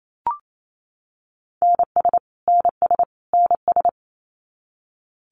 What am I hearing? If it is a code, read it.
Morse code, heard as a keyed tone, about 28 wpm, 700 Hz. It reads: NS NS NS
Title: Canadian Provinces and Territories - Repeated 3x - 28wpm